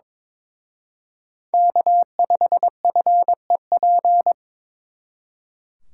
Morse code, heard as a keyed tone, about 26 wpm, 700 Hz. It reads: K5FEP
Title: Speed Racing - USA 1x3 Call Signs 22wpm